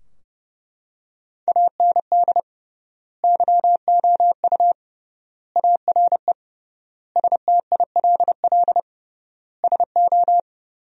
Morse code, heard as a keyed tone, about 32 wpm, 700 Hz. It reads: AND YOU ARE STILL SO
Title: 3x Word-Spacing - Sentences from Top 400 Words - 30wpm